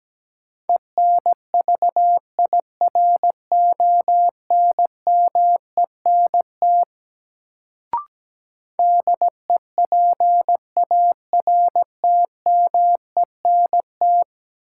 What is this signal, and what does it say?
Morse code, heard as a keyed tone, about 17 wpm, 700 Hz. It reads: ENVIRONMENT DEPARTMENT
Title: Sentences from Top 1000 Words - Review Part 5 - New Word Drill - 17wpm